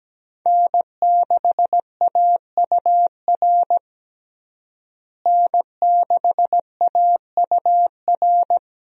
Morse code, heard as a keyed tone, about 17 wpm, 700 Hz. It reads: N6AUR N6AUR
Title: USA Call Signs - 5 Characters - Repeated 3x - 17wpm